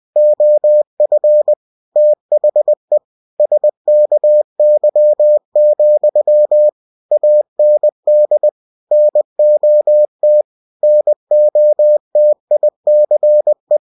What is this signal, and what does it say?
Morse code, heard as a keyed tone, about 20 wpm, 600 Hz. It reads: OF THE SKY, AND NOT NOTICE
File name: cw-books / winnie-the-pooh / winnie-20wpm-page-0005